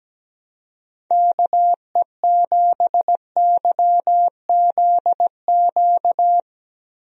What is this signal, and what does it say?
Morse code, heard as a keyed tone, about 17 wpm, 700 Hz. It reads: KE7YZQ
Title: USA Call Signs - 6 Characters - Repeated 2x - 17wpm